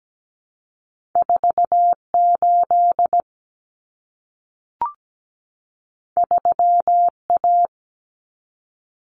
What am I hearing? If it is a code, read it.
Morse code, heard as a keyed tone, about 17 wpm, 700 Hz. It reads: 48 3A
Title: Common Cut Numbers - 2-Digit 17wpm